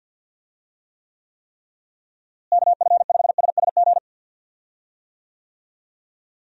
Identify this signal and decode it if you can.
Morse code, heard as a keyed tone, about 45 wpm, 700 Hz. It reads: KF5SSG
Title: USA Call Signs - 6 Characters - Repeated 3x - 50wpm